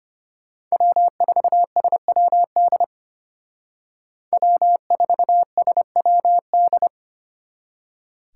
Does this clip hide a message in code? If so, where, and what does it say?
Yes: Morse code from 0.7 to 6.9 s, heard as a keyed tone, about 32 wpm, 700 Hz. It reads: W4SWD W4SWD